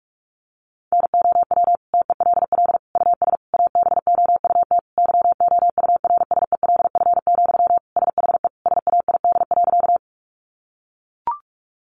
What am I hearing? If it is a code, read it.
Morse code, heard as a keyed tone, about 44 wpm, 700 Hz. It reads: NOW TELL US ABOUT YOURSELF, SHE SAID.
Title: Sentences from Top 600 Words 45wpm